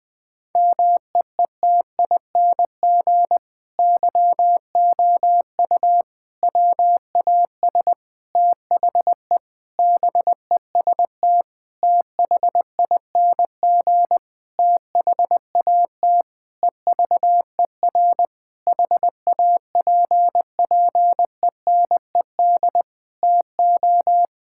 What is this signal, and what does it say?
Morse code, heard as a keyed tone, about 20 wpm, 700 Hz. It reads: MEETING YOU WAS THE BEST THING THAT EVER HAPPENED TO
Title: Sentences with 8-Word Phrases - 20wpm